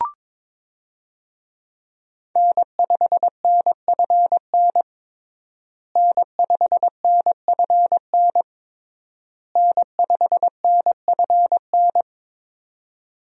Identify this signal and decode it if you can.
Morse code, heard as a keyed tone, about 22 wpm, 700 Hz. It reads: N5NFN N5NFN N5NFN
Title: USA Call Signs - 5 Characters - Repeated 3x - 22wpm